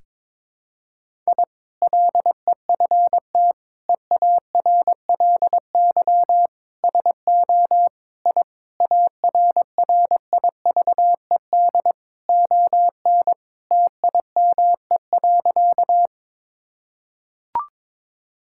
Morse code, heard as a keyed tone, about 22 wpm, 700 Hz. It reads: I LEFT EARLY SO I ARRIVED ON TIME.
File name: practice / 22wpm / Sentences with 2-Word Phrases - 22wpm